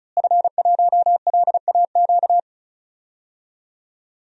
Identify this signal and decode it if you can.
Morse code, heard as a keyed tone, about 36 wpm, 700 Hz. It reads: F1LAQ
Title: DXCC-100 Call Signs - 5 Characters - Repeated 2x - 35wpm